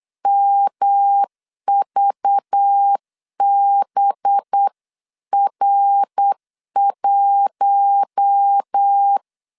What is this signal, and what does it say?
Morse code, heard as a keyed tone, about 8 wpm, 800 Hz. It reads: MVBR1
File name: CW / DET / 1 6 1500C S B 8wpm 5142